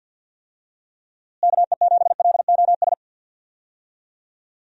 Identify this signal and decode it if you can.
Morse code, heard as a keyed tone, about 51 wpm, 700 Hz. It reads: KE7LOS
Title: USA Call Signs - 6 Characters - Repeated 3x - 50wpm